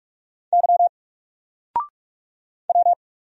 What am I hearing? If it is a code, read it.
Morse code, heard as a keyed tone, about 46 wpm, 700 Hz. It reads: Y W
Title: Single Letter-Number 45wpm